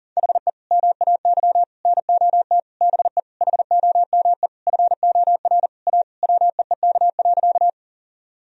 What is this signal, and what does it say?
Morse code, heard as a keyed tone, about 40 wpm, 700 Hz. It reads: HE MAY NOT BE HOME FOR A WEEK.
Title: Sentences from Top 1000 Words - Review Part 2 - 40wpm